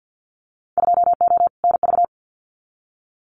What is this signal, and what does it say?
Morse code, heard as a keyed tone, about 50 wpm, 700 Hz. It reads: 2O NV
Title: Winter Field Day Exchanges 50wpm